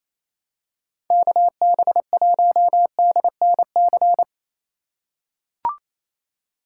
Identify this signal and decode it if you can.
Morse code, heard as a keyed tone, about 28 wpm, 700 Hz. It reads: KB1DNC
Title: USA Call Signs - 2x Character Spacing - 28wpm